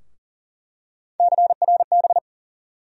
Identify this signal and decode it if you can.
Morse code, heard as a keyed tone, about 40 wpm, 700 Hz. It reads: CRB
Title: USA Call Sign Postfixes - 3 Letters 40wpm